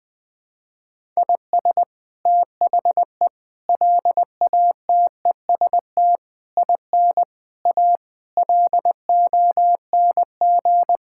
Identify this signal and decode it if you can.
Morse code, heard as a keyed tone, about 20 wpm, 700 Hz. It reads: IS THE LATEST IN A LONG
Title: Sets of 8 Words - 20wpm